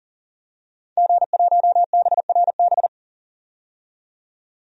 Morse code, heard as a keyed tone, about 38 wpm, 700 Hz. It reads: G1BRB
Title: DXCC-100 Call Signs - 5 Characters - 40wpm